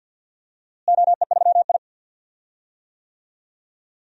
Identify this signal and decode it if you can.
Morse code, heard as a keyed tone, about 51 wpm, 700 Hz. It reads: OE3I
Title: DXCC-100 Call Signs - 4 Characters - Repeated 2x - 50wpm